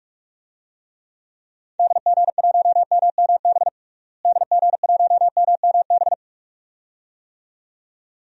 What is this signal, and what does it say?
Morse code, heard as a keyed tone, about 46 wpm, 700 Hz. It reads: DG1MMB DG1MMB
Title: DXCC-100 Call Signs - 6 Characters - Repeated 2x - 45wpm